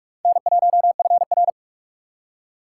Morse code, heard as a keyed tone, about 51 wpm, 700 Hz. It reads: N1FR
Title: Speed Racing - USA 1x2 Call Signs 45wpm